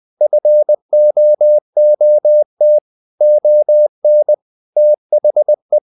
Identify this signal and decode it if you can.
Morse code, heard as a keyed tone, about 20 wpm, 600 Hz. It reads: FOOT ON THE
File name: the old man and the sea / the old man and the sea part 1 20wpm